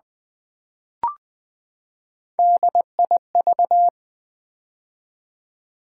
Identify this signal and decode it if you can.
Morse code, heard as a keyed tone, about 20 wpm, 700 Hz. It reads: DIV
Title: Build A Word - 3 Letters At A Time - 8-Letter Words - 20wpm